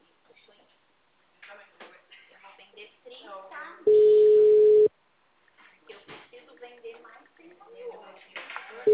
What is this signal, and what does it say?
Ringback tone (Europe, 425 Hz), with speech.